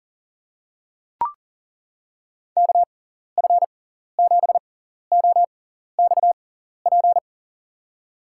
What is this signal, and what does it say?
Morse code, heard as a keyed tone, about 40 wpm, spaced as for 14 wpm, 700 Hz. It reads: KF7OXP
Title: USA Call Signs - 6 Characters - 6x Character Spacing - 40wpm